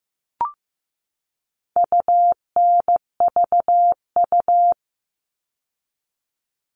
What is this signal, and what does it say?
Morse code, heard as a keyed tone, about 15 wpm, 700 Hz. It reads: UNVU
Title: All Cut Numbers - 4-Digit 15wpm